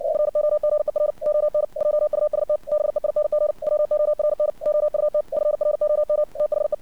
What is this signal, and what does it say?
Morse code, heard as a keyed tone, about 57 wpm, 600 Hz. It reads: FOGER ON JUST BEING WORN OUT FROM THE
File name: archived / CRSnet